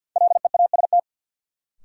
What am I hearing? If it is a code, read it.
Morse code, heard as a keyed tone, about 53 wpm, 700 Hz. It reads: LEAST
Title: Sentences from Top 700 Words - New Word Drill - 50wpm